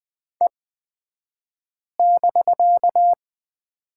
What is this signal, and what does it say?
Morse code, heard as a keyed tone, about 20 wpm, spaced as for 5 wpm, 700 Hz. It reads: E<BK>
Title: 249 Lesson 40d - BK - Two Characters 20wpm